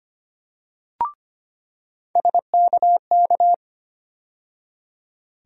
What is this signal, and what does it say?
Morse code, heard as a keyed tone, about 25 wpm, 700 Hz. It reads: SKK
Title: USA Call Sign Postfixes - 3 Letters 25wpm